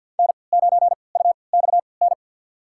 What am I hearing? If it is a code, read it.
Morse code, heard as a keyed tone, about 53 wpm, spaced as for 29 wpm, 700 Hz. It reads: N9UXN
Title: USA Call Signs - 3x Character Spacing - 50wpm